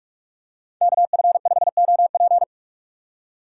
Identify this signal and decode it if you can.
Morse code, heard as a keyed tone, about 41 wpm, 700 Hz. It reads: KF5OP